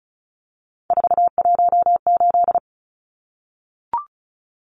Morse code, heard as a keyed tone, about 34 wpm, 700 Hz. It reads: 418
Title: Numbers - 3-Digit 35wpm